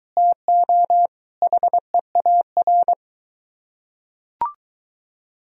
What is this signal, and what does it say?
Morse code, heard as a keyed tone, about 23 wpm, 700 Hz. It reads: TO HEAR